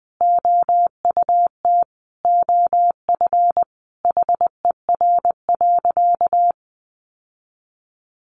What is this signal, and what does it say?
Morse code, heard as a keyed tone, about 20 wpm, 700 Hz. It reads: OUT OF HER.
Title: Sentences from Top 1000 Words 20wpm